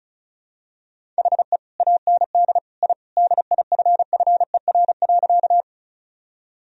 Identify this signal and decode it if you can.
Morse code, heard as a keyed tone, about 34 wpm, 700 Hz. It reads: HE AND I DIFFER.
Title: Sentences from Top 1000 Words - Review Part 1 - 35wpm